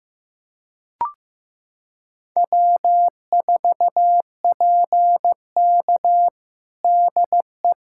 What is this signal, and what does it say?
Morse code, heard as a keyed tone, about 15 wpm, 700 Hz. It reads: W4PK DE
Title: USA Callsign DE Callsign - 4 Characters - 15wpm